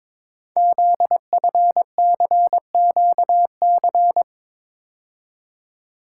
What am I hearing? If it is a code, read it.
Morse code, heard as a keyed tone, about 22 wpm, 700 Hz. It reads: ZFCQC
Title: Code Groups - 5 Letters - 22wpm